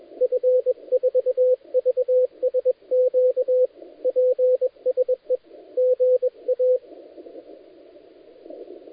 Morse code, heard as a keyed tone, about 23 wpm, 500 Hz, with radio noise. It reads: F4VSQ PSE GA
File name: dlazaawansowanych / nagraniazpasma / titawaNETcontrol